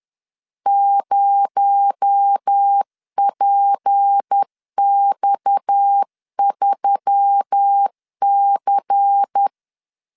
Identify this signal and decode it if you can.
Morse code, heard as a keyed tone, about 10 wpm, 800 Hz. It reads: 0PX3C
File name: LEZIONI / 6 PKX0 / 6 PKX0 10wpm 2707